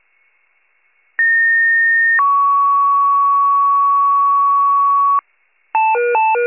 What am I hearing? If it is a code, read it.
Two-tone page, heard as a one-second tone, then a three-second tone, about 1850 then 1120 Hz, with noise. It is heard from 1.2 to 5.2 s.
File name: wcalert